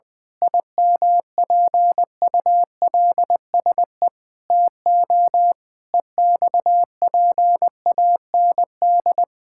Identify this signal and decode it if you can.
Morse code, heard as a keyed tone, about 20 wpm, 700 Hz. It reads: IMPULSE TO EXPAND
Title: Pangram Challenge v2 - 20wpm